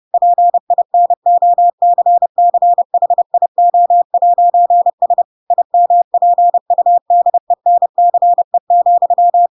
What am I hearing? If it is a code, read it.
Morse code, heard as a keyed tone, about 30 wpm, 700 Hz. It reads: PINOCCHIO'S IMPUDENCE,
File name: Pinocchio / Pinocchio ch4 30 30 ik1hge